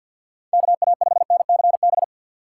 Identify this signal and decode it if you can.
Morse code, heard as a keyed tone, about 52 wpm, 700 Hz. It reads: KA5NCB
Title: USA Call Signs - 6 Characters - 6x Character Spacing - 50wpm